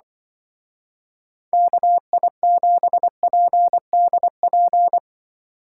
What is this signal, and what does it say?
Morse code, heard as a keyed tone, about 23 wpm, 700 Hz. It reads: KI7PDP
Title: Speed Racing - USA 2x3 Call Signs 20wpm